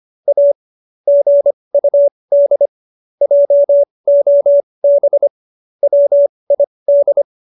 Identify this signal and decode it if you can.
Morse code, heard as a keyed tone, about 25 wpm, spaced as for 20 wpm, 570 Hz. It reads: A GUD JOB WID